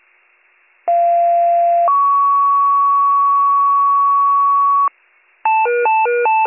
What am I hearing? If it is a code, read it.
Two-tone page, heard as a one-second tone, then a three-second tone, about 690 then 1120 Hz. It is heard from 0.9 to 4.9 s.